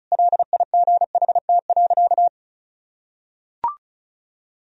Morse code, heard as a keyed tone, about 36 wpm, 700 Hz. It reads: LIGHT.
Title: Sentences from Top 1000 Words - Review Part 2 - 35wpm